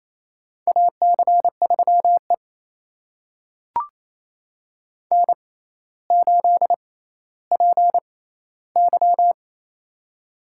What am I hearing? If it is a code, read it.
Morse code, heard as a keyed tone, about 28 wpm, spaced as for 14 wpm, 700 Hz. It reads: AC3E N8PY
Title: USA Call Signs - 4 Characters - 6x Character Spacing - 28wpm